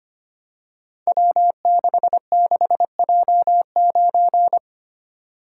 Morse code, heard as a keyed tone, about 25 wpm, 700 Hz. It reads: W66J9
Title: Code Groups - 5 Letter:Numbers - 25wpm